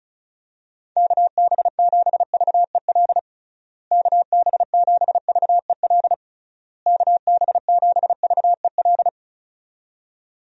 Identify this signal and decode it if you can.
Morse code, heard as a keyed tone, about 36 wpm, 700 Hz. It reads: KB7VEL KB7VEL KB7VEL
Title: USA Call Signs - 6 Characters - Repeated 3x - 35wpm